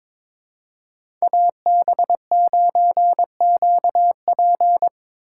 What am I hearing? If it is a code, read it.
Morse code, heard as a keyed tone, about 23 wpm, 700 Hz. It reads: AB9QP
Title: USA Call Signs - 5 Characters - 2x Character Spacing - 22wpm